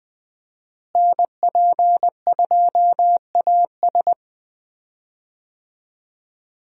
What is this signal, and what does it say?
Morse code, heard as a keyed tone, about 20 wpm, 700 Hz. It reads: NP2AS
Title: USA Call Signs - 5 Characters - Repeated 3x - 20wpm